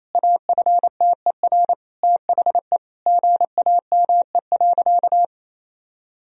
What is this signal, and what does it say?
Morse code, heard as a keyed tone, about 28 wpm, 700 Hz. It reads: AFTER THE GAME.
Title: Sentences from Top 1000 Words - Review Part 4 - 28wpm